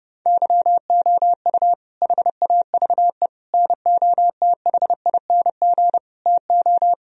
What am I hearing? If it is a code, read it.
Morse code, heard as a keyed tone, about 30 wpm, 700 Hz. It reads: YOU HAVE NOTHING TO
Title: Sentences from Top 1000 Words - Review Part 1 - 30wpm